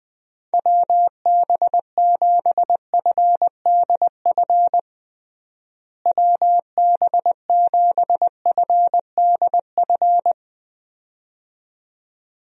Morse code, heard as a keyed tone, about 20 wpm, 700 Hz. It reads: WB7FDF WB7FDF
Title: USA Call Signs - 6 Characters - Repeated 2x - 20wpm